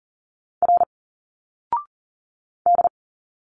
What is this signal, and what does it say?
Morse code, heard as a keyed tone, about 40 wpm, 700 Hz. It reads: R D
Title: Single Letters 40wpm